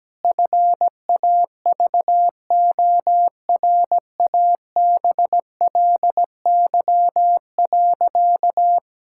Morse code, heard as a keyed tone, about 17 wpm, 700 Hz. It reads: FAVORABLY.
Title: Pangram Challenge v2 - 17wpm